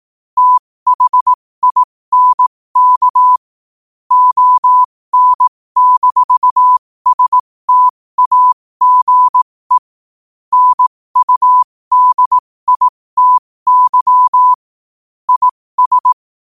Morse code, heard as a keyed tone, about 18 wpm, spaced as for 15 wpm, 1000 Hz. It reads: THINK ON-STAGE NUDITY IS
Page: New quotes every day in morse code at 15 Words per minute.